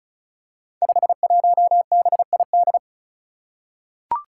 Morse code, heard as a keyed tone, about 36 wpm, 700 Hz. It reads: 51BID
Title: Five Challenging Letter-Number Combinations 35wpm